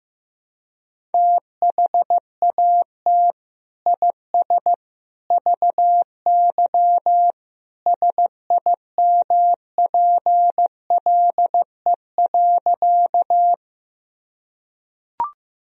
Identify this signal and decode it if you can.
Morse code, heard as a keyed tone, about 15 wpm, 700 Hz. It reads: THAT IS VY SIMPLE.
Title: Sentences from Top 400 Words and Abbreviations 15wpm